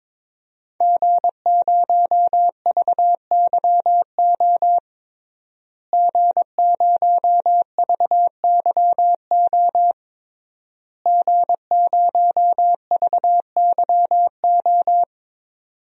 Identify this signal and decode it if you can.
Morse code, heard as a keyed tone, about 22 wpm, 700 Hz. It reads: G0VYO G0VYO G0VYO
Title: DXCC-100 Call Signs - 5 Characters - Repeated 3x - 22wpm